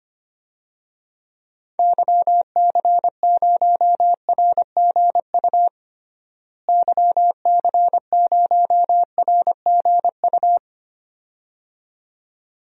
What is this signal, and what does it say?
Morse code, heard as a keyed tone, about 25 wpm, 700 Hz. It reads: YC0RGU YC0RGU
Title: DXCC-100 Call Signs - 6 Characters - Repeated 2x - 25wpm